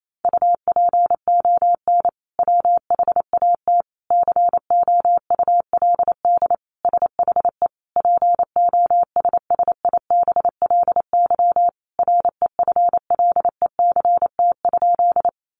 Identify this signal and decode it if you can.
Morse code, heard as a keyed tone, about 28 wpm, 700 Hz. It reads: UPON WHAT COULD SHE POSSIBLY REFLECT?
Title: Sentences from Top 1500 Words 28wpm